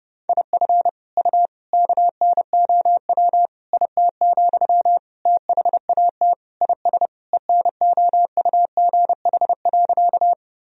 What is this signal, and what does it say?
Morse code, heard as a keyed tone, about 30 wpm, 700 Hz. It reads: IF U KNOW IT, THAT IS ENOUGH.
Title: Sentences from Top 400 Words and Abbreviations 30wpm